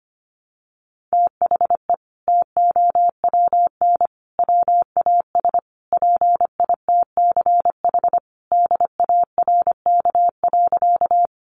Morse code, heard as a keyed tone, about 25 wpm, 700 Hz. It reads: THE TOWN WAS PITCH DARK.
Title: Sentences from Top 1300 Words 25wpm